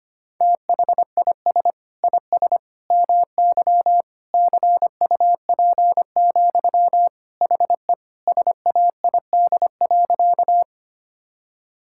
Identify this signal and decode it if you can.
Morse code, heard as a keyed tone, about 25 wpm, 700 Hz. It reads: THIS IS MY CUP, HE SAID.
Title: Sentences from Top 1600 Words 25wpm